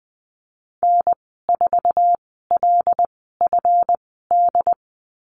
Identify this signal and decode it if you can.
Morse code, heard as a keyed tone, about 19 wpm, spaced as for 14 wpm, 700 Hz. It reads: N4LFD
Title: 154 Lesson 25g - Letter p - Call Signs 20wpm